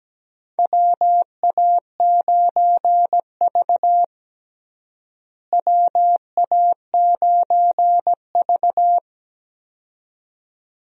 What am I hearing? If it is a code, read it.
Morse code, heard as a keyed tone, about 17 wpm, 700 Hz. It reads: WA9V WA9V
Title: USA Call Signs - 4 Characters - Repeated 2x - 17wpm